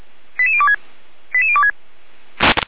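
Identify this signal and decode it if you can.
Five-tone selective call (ZVEI), heard as quick runs of five tones, tones of about 70 ms. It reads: 88026 88026
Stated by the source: ZVEI five-tone selective call 88124 received by SDR, logged by Feuerwehr Orlamünde fire brigade.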